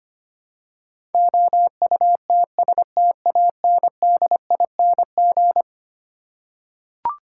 Morse code, heard as a keyed tone, about 25 wpm, 700 Hz. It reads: OUTSTANDING
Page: morse-code-ninja-practice-25wpm